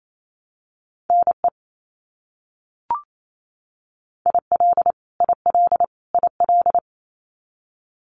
Morse code, heard as a keyed tone, about 28 wpm, 700 Hz. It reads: NE IL IL IL
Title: US States Repeated 3x 28wpm